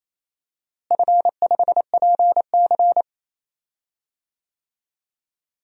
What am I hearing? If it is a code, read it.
Morse code, heard as a keyed tone, about 28 wpm, 700 Hz. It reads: F5PC
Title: DXCC-100 Call Signs - 4 Characters - Repeated 3x - 28wpm